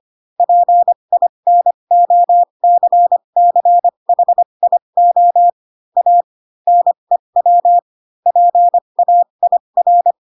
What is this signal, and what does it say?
Morse code, heard as a keyed tone, about 25 wpm, spaced as for 22 wpm, 700 Hz. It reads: PINOCCHIO A NEW PAIR
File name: Pinocchio / Pinocchio ch8 18 25 ik1hge